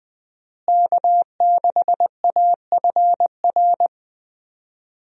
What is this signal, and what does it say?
Morse code, heard as a keyed tone, about 20 wpm, 700 Hz. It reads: K6AFR
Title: USA Call Signs - 5 Characters - Repeated 3x - 20wpm